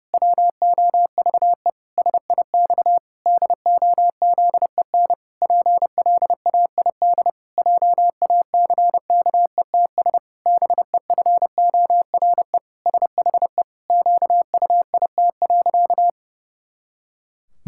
Morse code, heard as a keyed tone, about 31 wpm, 700 Hz. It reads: WOVE SIX DOZEN PLAID JACKETS BEFORE SHE QUIT.
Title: Pangram Challenge v2 - 30wpm